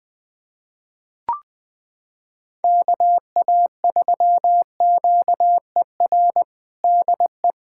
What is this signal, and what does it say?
Morse code, heard as a keyed tone, about 20 wpm, 700 Hz. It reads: KA3QER DE
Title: USA Callsign DE Callsign - 6 Characters - 20wpm